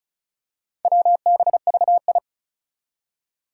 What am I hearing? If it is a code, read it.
Morse code, heard as a keyed tone, about 36 wpm, 700 Hz. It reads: WBVI
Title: Four Challenging Letter-Number Combinations 35wpm